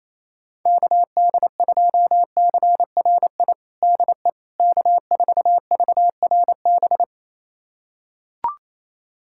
Morse code, heard as a keyed tone, about 28 wpm, 700 Hz. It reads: KD2CRI DE K4VRB
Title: USA Callsign DE Callsign 28wpm